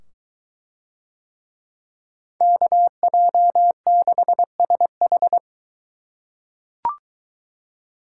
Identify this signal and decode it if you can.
Morse code, heard as a keyed tone, about 22 wpm, 700 Hz. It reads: KJ6SH